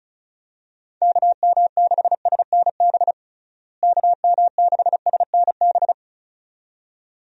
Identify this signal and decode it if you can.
Morse code, heard as a keyed tone, about 36 wpm, 700 Hz. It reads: KM6SNB KM6SNB
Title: USA Call Signs - 6 Characters - Repeated 2x - 35wpm